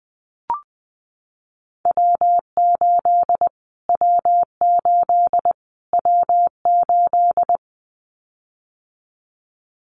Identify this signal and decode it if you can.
Morse code, heard as a keyed tone, about 20 wpm, 700 Hz. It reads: W8 W8 W8
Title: Callsign Prefixes for Top 25 DXCC - Repeated 3x - 20wpm